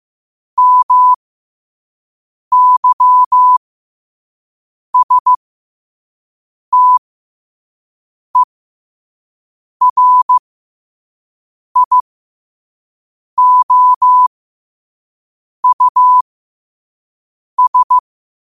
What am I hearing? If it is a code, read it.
Morse code, heard as a keyed tone, about 15 wpm, spaced as for 5 wpm, 1000 Hz. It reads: MYSTERIOUS
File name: qotdfiles / 5wpm